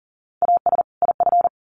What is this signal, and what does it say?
Morse code, heard as a keyed tone, about 40 wpm, 700 Hz. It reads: AS IF
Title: Sets of 2 Words 40wpm